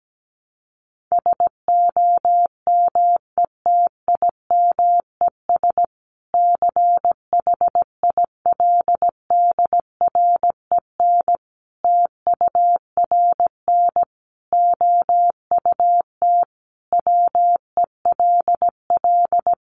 Morse code, heard as a keyed tone, about 17 wpm, 700 Hz. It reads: SOMETIMES CHILDREN TURN OUT WELL
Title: Sentences from Top 600 Words 17wpm